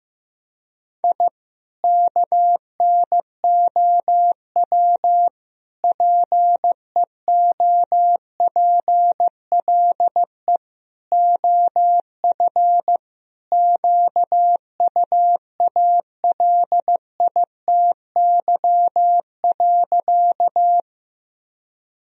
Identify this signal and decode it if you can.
Morse code, heard as a keyed tone, about 15 wpm, 700 Hz. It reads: I KNOW PEOPLE OF QUALITY.
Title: Sentences from Top 1000 Words - Review Part 4 - 15wpm